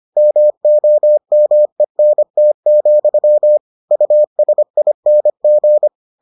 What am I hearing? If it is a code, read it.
Morse code, heard as a keyed tone, about 25 wpm, 600 Hz. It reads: MOMENT, USING